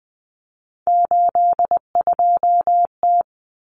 Morse code, heard as a keyed tone, about 20 wpm, 700 Hz. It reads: 82T
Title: Common Cut Numbers - 3-Digit 20wpm